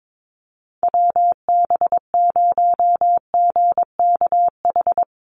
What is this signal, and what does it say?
Morse code, heard as a keyed tone, about 22 wpm, 700 Hz. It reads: WB0GKH
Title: Call Signs Repeated 3x - 22wpm